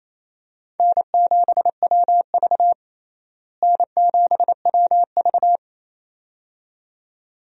Morse code, heard as a keyed tone, about 27 wpm, 700 Hz. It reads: N7WV N7WV
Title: USA Call Signs - 4 Characters - Repeated 2x - 28wpm